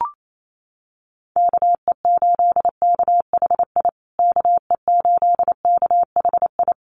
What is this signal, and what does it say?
Morse code, heard as a keyed tone, about 28 wpm, 700 Hz. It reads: KE8KHI KE8KHI
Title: USA Call Signs Repeated 2x 28wpm